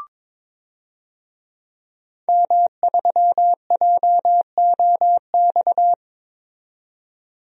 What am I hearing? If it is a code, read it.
Morse code, heard as a keyed tone, about 22 wpm, 700 Hz. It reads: M3JOX
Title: DXCC-100 Call Signs - 5 Characters - 22wpm